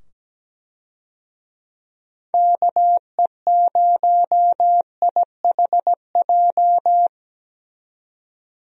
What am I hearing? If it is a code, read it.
Morse code, heard as a keyed tone, about 17 wpm, 700 Hz. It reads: KE0IHJ
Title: USA Call Signs - 6 Characters - 17wpm